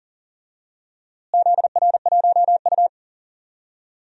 Morse code, heard as a keyed tone, about 37 wpm, 700 Hz. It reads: ZR1U
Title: DXCC-100 Call Signs - 4 Characters - Repeated 2x - 40wpm